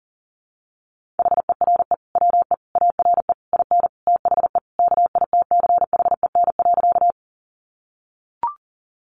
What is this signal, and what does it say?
Morse code, heard as a keyed tone, about 40 wpm, 700 Hz. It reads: HERE WE ARE IN THE KITCHEN.
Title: Sentences from Top 1400 Words 40wpm